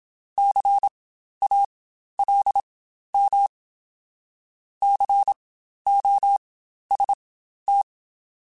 Morse code, heard as a keyed tone, about 27 wpm, spaced as for 12 wpm, 800 Hz. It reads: CALM COST